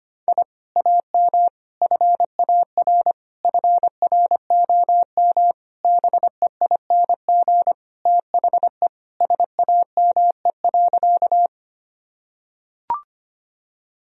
Morse code, heard as a keyed tone, about 25 wpm, 700 Hz. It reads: I AM FAR FROM BEING THE SAME.
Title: Sentences from Top 1000 Words - Review Part 3 - 25wpm